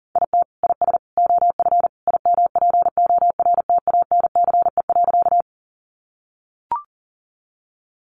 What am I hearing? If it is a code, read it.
Morse code, heard as a keyed tone, about 37 wpm, 700 Hz. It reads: IT IS OF IMPORTANCE.
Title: Sentences from Top 1200 Words 40wpm